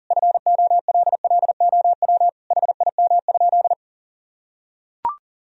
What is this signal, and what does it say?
Morse code, heard as a keyed tone, about 40 wpm, 700 Hz. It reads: FOLLOW HIM?
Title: Sentences from Top 1000 Words - Review Part 1 - 40wpm